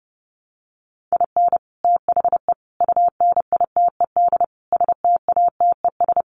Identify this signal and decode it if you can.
Morse code, heard as a keyed tone, about 28 wpm, 700 Hz. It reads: IN THE UNITED STATES
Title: Sets of 5 Words Encore2 30wpm